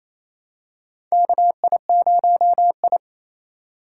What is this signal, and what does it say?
Morse code, heard as a keyed tone, about 27 wpm, 700 Hz. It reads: KI0I
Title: K1USN SST QSO Practice - 28wpm